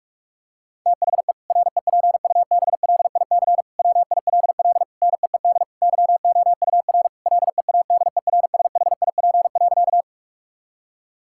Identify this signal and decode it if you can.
Morse code, heard as a keyed tone, about 41 wpm, 700 Hz. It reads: THE REPUBLIC WILL NEED YOUR LEADERSHIP.
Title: Sentences from Top 1600 Words 45wpm